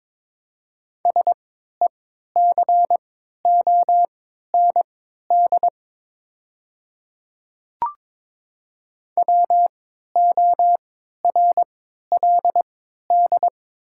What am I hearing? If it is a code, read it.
Morse code, heard as a keyed tone, about 22 wpm, spaced as for 13 wpm, 700 Hz. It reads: SECOND WORLD
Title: 3x Character Spacing - Top 300 Words - 22wpm